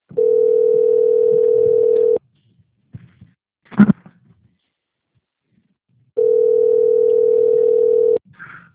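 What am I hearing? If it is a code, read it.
Ringback tone (North America), 440 + 480 Hz.
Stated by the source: Telephone call recorded on the Presidente Prudente 4 phone system.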